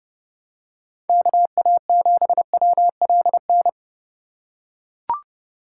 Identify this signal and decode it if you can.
Morse code, heard as a keyed tone, about 29 wpm, 700 Hz. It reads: KA7WLN
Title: USA Call Signs - 3x Character Spacing - 30wpm